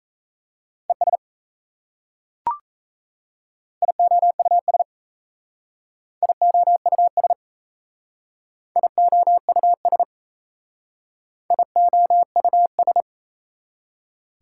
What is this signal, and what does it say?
Morse code, heard as a keyed tone, about 35 wpm, 700 Hz. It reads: ES IOUS IOUS IOUS IOUS